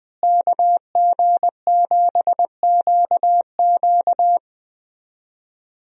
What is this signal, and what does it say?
Morse code, heard as a keyed tone, about 21 wpm, 700 Hz. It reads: KG7QQ
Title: USA Call Signs - 5 Characters - Repeated 2x - 20wpm